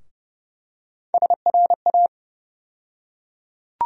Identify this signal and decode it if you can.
Morse code, heard as a keyed tone, about 30 wpm, 700 Hz. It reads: SRA
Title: USA Call Sign Postfixes - 3 Letters 30wpm